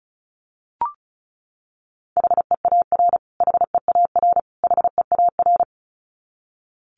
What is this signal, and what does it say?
Morse code, heard as a keyed tone, about 34 wpm, 700 Hz. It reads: HEAR HEAR HEAR
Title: Top 300 Words Repeated 35wpm